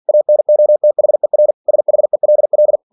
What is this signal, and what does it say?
Morse code, heard as a keyed tone, about 46 wpm, 600 Hz. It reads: ANOTHER SHELL
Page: Never 47wpm